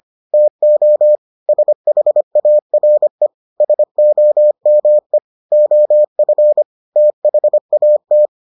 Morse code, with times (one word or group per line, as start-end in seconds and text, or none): 0.3-1.2 TO
1.5-3.3 SHARE
3.6-5.2 SOME
5.5-6.6 OF
7.0-8.3 THAT